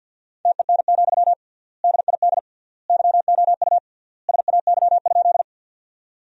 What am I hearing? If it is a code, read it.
Morse code, heard as a keyed tone, about 50 wpm, 700 Hz. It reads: TEN, DID YOU SAY?
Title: 3x Word-Spacing - Sentences from Top 400 Words - 50wpm